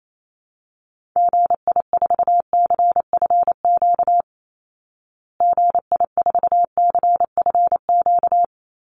Morse code, heard as a keyed tone, about 28 wpm, 700 Hz. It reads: GI4CFQ GI4CFQ
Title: Call Signs Repeated 3x - 28wpm